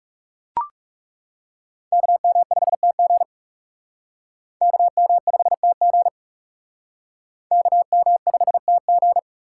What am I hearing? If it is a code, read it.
Morse code, heard as a keyed tone, about 37 wpm, 700 Hz. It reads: KM5TG KM5TG KM5TG
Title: Speed Racing - USA 2x2 Call Signs 30wpm